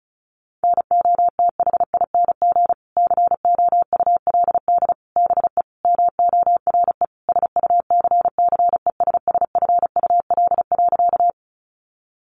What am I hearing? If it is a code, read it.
Morse code, heard as a keyed tone, about 34 wpm, 700 Hz. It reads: NOTHING COULD BE MORE SUCCESSFUL.
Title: Sentences from Top 900 Words 35wpm